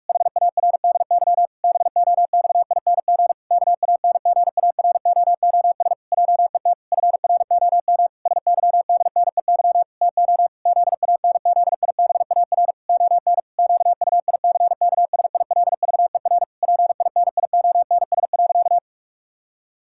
Morse code, heard as a keyed tone, about 41 wpm, 700 Hz. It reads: HARDY BOXING KANGAROOS JET FROM SYDNEY TO ZANZIBAR ON QUICKSILVER PINIONS.
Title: Pangram Challenge v2 - 45wpm